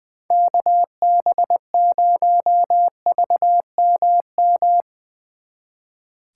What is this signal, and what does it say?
Morse code, heard as a keyed tone, about 22 wpm, 700 Hz. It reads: KB0VMM
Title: Speed Racing - USA Call Signs 20wpm